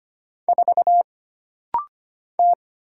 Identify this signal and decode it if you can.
Morse code, heard as a keyed tone, about 25 wpm, 700 Hz. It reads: 4 T
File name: Single Character (Letter:Number:Punctuation:Prosign) - Rapid-Fire - 25wpm